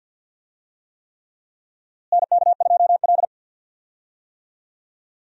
Morse code, heard as a keyed tone, about 47 wpm, 700 Hz. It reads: NK2L
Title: USA Call Signs - 4 Characters - Repeated 3x - 50wpm